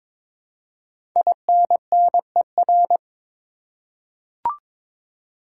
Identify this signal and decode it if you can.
Morse code, heard as a keyed tone, about 22 wpm, 700 Hz. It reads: INNER
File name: 5 Letter Words - First 3 Letters Given - 22wpm